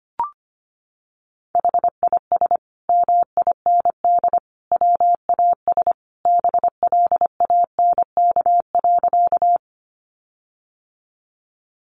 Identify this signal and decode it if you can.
Morse code, heard as a keyed tone, about 25 wpm, 700 Hz. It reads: HIS MIND WAS BLANK.
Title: Sentences from Top 1500 Words 25wpm